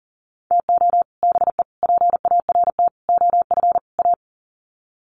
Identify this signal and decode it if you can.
Morse code, heard as a keyed tone, about 40 wpm, 700 Hz. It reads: TO BE PART OF A
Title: Sets of 5 Words Encore2 40wpm